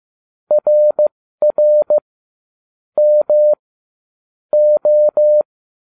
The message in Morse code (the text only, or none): RR M O